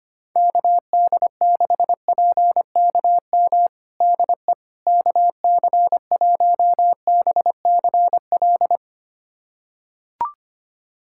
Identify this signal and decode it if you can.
Morse code, heard as a keyed tone, about 25 wpm, 700 Hz. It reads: KD6PKM DE KC1BCL
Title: USA Callsign DE Callsign 25wpm